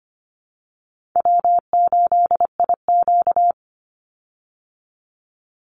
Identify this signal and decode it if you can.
Morse code, heard as a keyed tone, about 25 wpm, 700 Hz. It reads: W8IQ
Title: Call Signs Repeated 3x - 25wpm